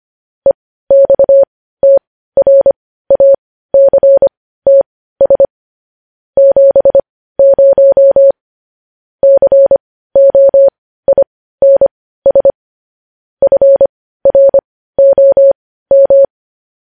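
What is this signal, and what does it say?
Morse code, heard as a keyed tone, about 25 wpm, spaced as for 15 wpm, 550 Hz. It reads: EXTRACTS 70 COINS FROM
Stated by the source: Morse code news headlines at 15 wpm.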